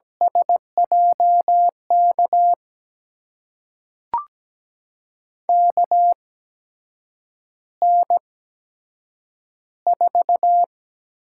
Morse code, heard as a keyed tone, about 17 wpm, spaced as for 7 wpm, 700 Hz. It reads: SJK KN4
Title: USA Call Signs - 6 Characters - 8x Character Spacing - 17wpm